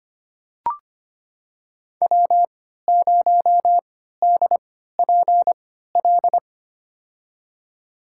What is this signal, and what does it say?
Morse code, heard as a keyed tone, about 25 wpm, spaced as for 14 wpm, 700 Hz. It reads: W0DPL
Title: USA Call Signs - 5 Characters - 3x Character Spacing - 25wpm